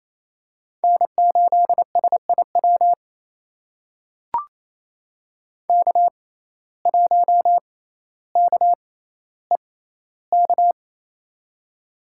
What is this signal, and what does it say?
Morse code, heard as a keyed tone, about 27 wpm, spaced as for 14 wpm, 700 Hz. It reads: N8SIW K1KEK
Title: USA Call Signs - 5 Characters - 6x Character Spacing - 28wpm